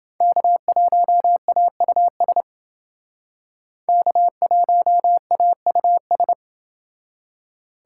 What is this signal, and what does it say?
Morse code, heard as a keyed tone, about 30 wpm, 700 Hz. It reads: K1AUS K1AUS